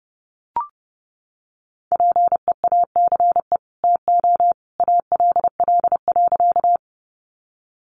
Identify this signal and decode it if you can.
Morse code, heard as a keyed tone, about 28 wpm, 700 Hz. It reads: PEACE TO ALL.
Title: Sentences from Top 1500 Words 30wpm